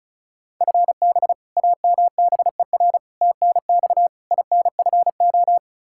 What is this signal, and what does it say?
Morse code, heard as a keyed tone, about 36 wpm, 700 Hz. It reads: FB AMBER TNX INFO
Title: Practice QSO - Exchange 3 - More Info - 35wpm